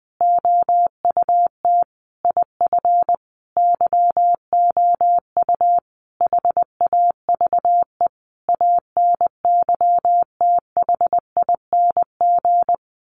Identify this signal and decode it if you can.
Morse code, heard as a keyed tone, about 20 wpm, 700 Hz. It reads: OUT IF YOU HAVE ANYTHING
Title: Sentences from Top 1300 Words 20wpm